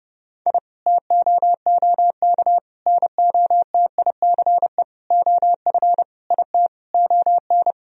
Morse code, heard as a keyed tone, about 32 wpm, 700 Hz. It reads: I TOOK NOTICE OF IT ON
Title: Sentences from Top 1000 Words - Review Part 2 - 30wpm